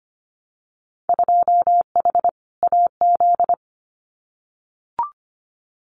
Morse code, heard as a keyed tone, about 25 wpm, 700 Hz. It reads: 2H AZ
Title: Winter Field Day Exchanges 25wpm